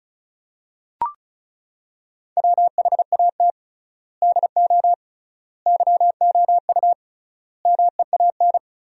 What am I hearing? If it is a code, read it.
Morse code, heard as a keyed tone, about 34 wpm, 700 Hz. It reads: WHAT DO YOU MEAN
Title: 3x Word-Spacing - Sentences from Top 200 Words - 35wpm